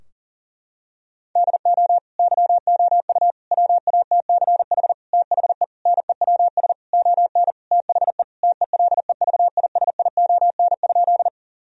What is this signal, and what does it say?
Morse code, heard as a keyed tone, about 40 wpm, 700 Hz. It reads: DO YOU WATCH THE NEWS ON THE TELEVISION?
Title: Sentences from Top 1000 Words - Review Part 4 - 40wpm